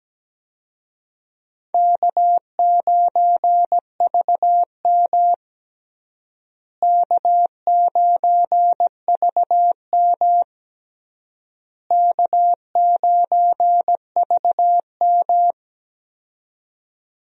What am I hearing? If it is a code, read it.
Morse code, heard as a keyed tone, about 17 wpm, 700 Hz. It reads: K9VM K9VM K9VM
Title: USA Call Signs - 4 Characters - Repeated 3x - 17wpm